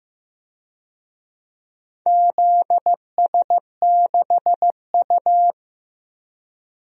Morse code, heard as a keyed tone, about 15 wpm, 700 Hz. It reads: ZS6U
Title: DXCC-100 Call Signs - 4 Characters - Repeated 2x - 15wpm